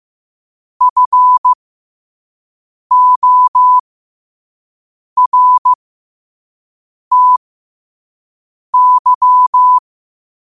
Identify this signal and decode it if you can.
Morse code, heard as a keyed tone, about 15 wpm, spaced as for 5 wpm, 1000 Hz. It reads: FORTY
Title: 5 WPM morse code quotes for Fri, 01 Mar 2024